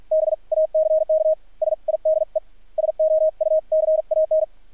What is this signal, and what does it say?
Morse code, heard as a keyed tone, about 50 wpm, spaced as for 42 wpm, 630 Hz. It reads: BACK SIDE SOUCAN
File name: archived / CRSnet